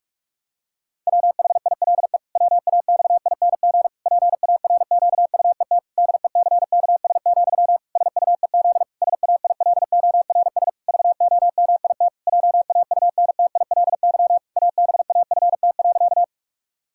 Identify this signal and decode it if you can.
Morse code, heard as a keyed tone, about 46 wpm, 700 Hz. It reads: WHILE WAXING PARQUET DECKS, SUEZ SAILORS VOMIT JAUNTILY ABAFT.
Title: Pangram Challenge v2 - 45wpm